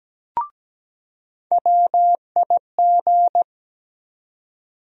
Morse code, heard as a keyed tone, about 17 wpm, 700 Hz. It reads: WIG
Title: USA Call Sign Postfixes - 3 Letters 17wpm